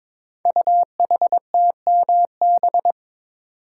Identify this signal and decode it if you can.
Morse code, heard as a keyed tone, about 22 wpm, 700 Hz. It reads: UHTMB
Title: Code Groups - 5 Letter:Numbers - 22wpm